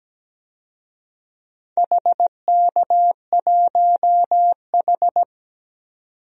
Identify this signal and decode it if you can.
Morse code, heard as a keyed tone, about 17 wpm, 700 Hz. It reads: HK1H
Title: DXCC-100 Call Signs - 4 Characters - 17wpm